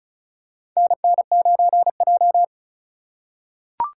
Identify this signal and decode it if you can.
Morse code, heard as a keyed tone, about 36 wpm, 700 Hz. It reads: NN9J
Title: USA Call Signs - 3x Character Spacing - 35wpm